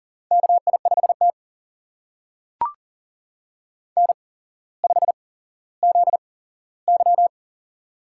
Morse code, heard as a keyed tone, about 40 wpm, spaced as for 17 wpm, 700 Hz. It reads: KI5T N5ZY
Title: USA Call Signs - 4 Characters - 8x Character Spacing - 40wpm